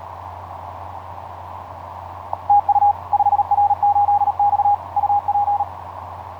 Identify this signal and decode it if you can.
Morse code, heard as a keyed tone, about 38 wpm, 820 Hz. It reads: TU 5R8X UP
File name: 5r8x iz5ebl etc 12m cw 07nov2010 1003